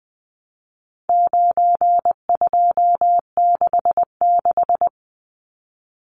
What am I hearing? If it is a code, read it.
Morse code, heard as a keyed tone, about 20 wpm, 700 Hz. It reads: 9266
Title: Numbers - 4-Digit 20wpm